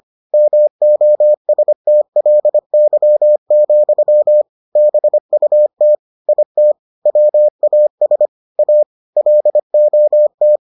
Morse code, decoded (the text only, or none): MOSTLY, BUT IT WAS A LOT